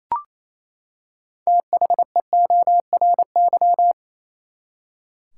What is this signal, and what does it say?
Morse code, heard as a keyed tone, about 28 wpm, 700 Hz. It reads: THEORY
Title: Six Letter Words - 28wpm